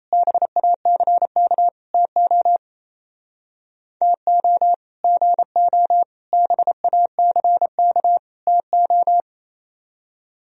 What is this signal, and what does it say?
Morse code, heard as a keyed tone, about 36 wpm, 700 Hz. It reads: BACK TO TO GO BACK TO